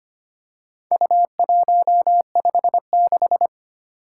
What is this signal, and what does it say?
Morse code, heard as a keyed tone, about 25 wpm, 700 Hz. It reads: U156
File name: practice / 25wpm / Four Letter-Number Combinations - 25wpm